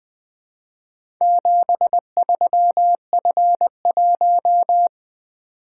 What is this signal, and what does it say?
Morse code, heard as a keyed tone, about 20 wpm, 700 Hz. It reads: 73F1
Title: Four Letter-Number Combinations - 20wpm